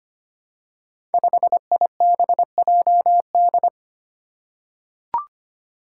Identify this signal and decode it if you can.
Morse code, heard as a keyed tone, about 25 wpm, 700 Hz. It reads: 5IBJD
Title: Five Challenging Letter-Number Combinations 25wpm